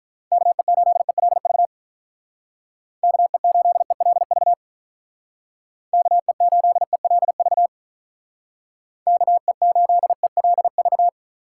Morse code, heard as a keyed tone, about 43 wpm, 700 Hz. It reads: KE8ELV KE8ELV KE8ELV KE8ELV
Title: Speed Racing - USA 2x3 Call Signs 35wpm